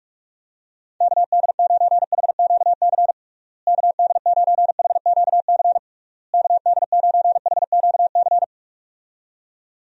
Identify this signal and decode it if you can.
Morse code, heard as a keyed tone, about 46 wpm, 700 Hz. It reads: KD9HQC KD9HQC KD9HQC
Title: USA Call Signs - 6 Characters - Repeated 3x - 45wpm